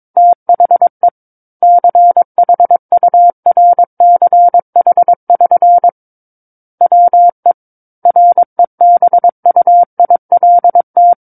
Morse code, heard as a keyed tone, about 22 wpm, 700 Hz. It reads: THE CHURCH<SN> WE REBUILT
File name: Book One / 13-Chapter 13 - 22 WPM